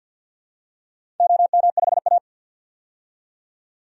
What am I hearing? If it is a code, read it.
Morse code, heard as a keyed tone, about 52 wpm, 700 Hz. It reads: OM5A
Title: DXCC-100 Call Signs - 4 Characters - 50wpm